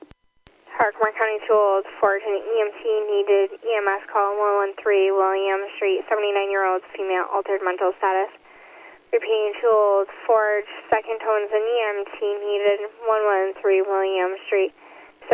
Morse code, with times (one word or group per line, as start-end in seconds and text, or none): none